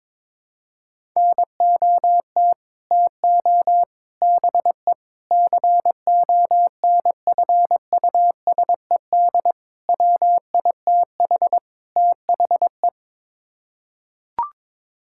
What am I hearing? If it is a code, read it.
Morse code, heard as a keyed tone, about 22 wpm, 700 Hz. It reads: NOT TO BE CONFUSED WITH THE
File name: practice / 22wpm / Sets of 6 Words - 22wpm